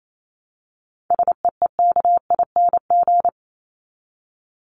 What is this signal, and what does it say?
Morse code, heard as a keyed tone, about 28 wpm, 700 Hz. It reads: SEEKING
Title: ING Suffix - Top 100 Words with Base Words 28wpm